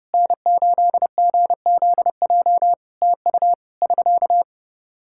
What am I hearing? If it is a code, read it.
Morse code, heard as a keyed tone, about 30 wpm, 700 Hz. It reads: N8GZJ TU <SK>
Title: Practice QSO - Exchange 4 - Ending - 30wpm